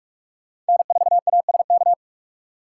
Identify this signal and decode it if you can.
Morse code, heard as a keyed tone, about 46 wpm, 700 Hz. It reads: N4ASK
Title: USA Call Signs - 5 Characters - 4x Character Spacing - 45wpm